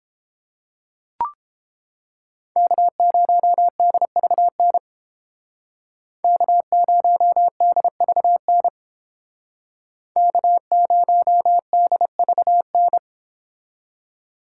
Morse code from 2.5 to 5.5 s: K0DVN

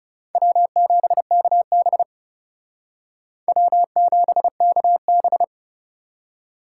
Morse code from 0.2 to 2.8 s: W7KB

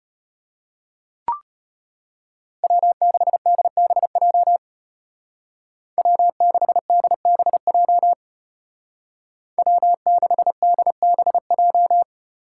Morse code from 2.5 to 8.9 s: W6DBJ W6DBJ

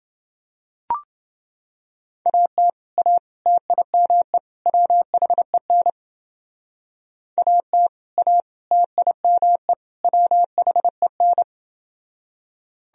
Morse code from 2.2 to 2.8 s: AT

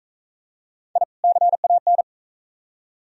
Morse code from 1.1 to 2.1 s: CAN

